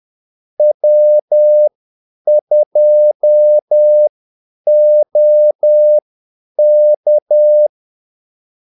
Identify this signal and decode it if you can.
Morse code, heard as a keyed tone, about 10 wpm, spaced as for 8 wpm, 600 Hz. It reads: W2OK
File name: Callsigns-10-8wpm